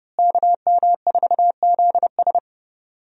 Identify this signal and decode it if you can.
Morse code, heard as a keyed tone, about 32 wpm, 700 Hz. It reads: KM4ZS
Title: USA Call Signs - 5 Characters - Repeated 2x - 30wpm